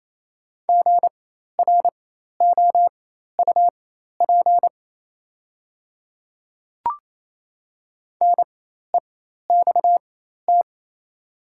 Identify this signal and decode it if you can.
Morse code, heard as a keyed tone, about 28 wpm, spaced as for 13 wpm, 700 Hz. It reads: GROUP NEXT
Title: 4x Character Spacing - Top 500 Words - 28wpm